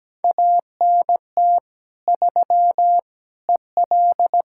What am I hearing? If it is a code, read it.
Morse code, heard as a keyed tone, about 17 wpm, 700 Hz. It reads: ANT 3 EL
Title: Practice QSO - Exchange 2 - Info - 17wpm